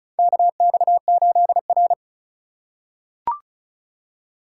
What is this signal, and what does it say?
Morse code, heard as a keyed tone, about 36 wpm, 700 Hz. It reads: KX8R
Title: USA Call Signs - 4 Characters - 2x Character Spacing - 35wpm